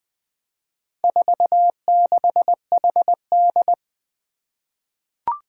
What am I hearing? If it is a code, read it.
Morse code, heard as a keyed tone, about 20 wpm, 700 Hz. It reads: 46HD
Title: Four Challenging Letter-Number Combinations 20wpm